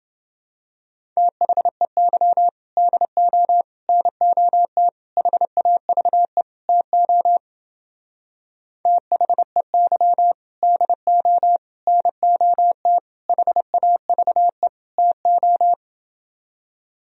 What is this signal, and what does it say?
Morse code, heard as a keyed tone, about 30 wpm, 700 Hz. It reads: THEY DO NOT HAVE TO THEY DO NOT HAVE TO